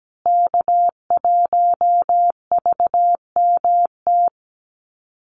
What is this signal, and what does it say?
Morse code, heard as a keyed tone, about 17 wpm, 700 Hz. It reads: K1VMT
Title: Call Signs Repeated 2x - 17wpm